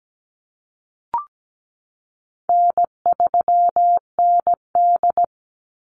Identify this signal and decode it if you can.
Morse code, heard as a keyed tone, about 17 wpm, 700 Hz. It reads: N3ND
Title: Easy Call Signs Repeated 3x - 17wpm